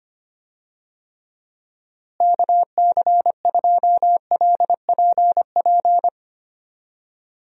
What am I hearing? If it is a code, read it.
Morse code, heard as a keyed tone, about 25 wpm, 700 Hz. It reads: KC2LPP
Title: USA Call Signs - 6 Characters - Repeated 3x - 25wpm